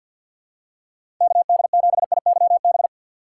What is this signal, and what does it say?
Morse code, heard as a keyed tone, about 53 wpm, 700 Hz. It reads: KD7IYB
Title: USA Call Signs - 6 Characters - 6x Character Spacing - 50wpm